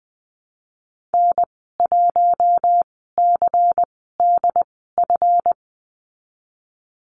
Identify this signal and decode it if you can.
Morse code, heard as a keyed tone, about 19 wpm, spaced as for 14 wpm, 700 Hz. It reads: N1CDF
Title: 147 Lesson 24g - Letter y - Call Signs 20wpm